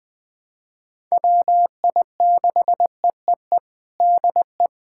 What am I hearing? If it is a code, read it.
Morse code, heard as a keyed tone, about 20 wpm, 700 Hz. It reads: WI6EEE DE
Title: USA Callsign DE Callsign - 6 Characters - 20wpm